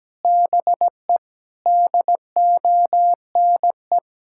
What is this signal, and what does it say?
Morse code, heard as a keyed tone, about 17 wpm, 700 Hz. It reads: BE DONE
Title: Sentences from Top 1600 Words 17wpm